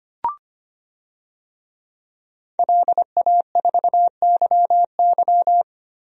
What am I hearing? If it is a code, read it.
Morse code, heard as a keyed tone, about 25 wpm, 700 Hz. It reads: LA4YY
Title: DXCC-100 Call Signs - 5 Characters - Repeated 2x - 25wpm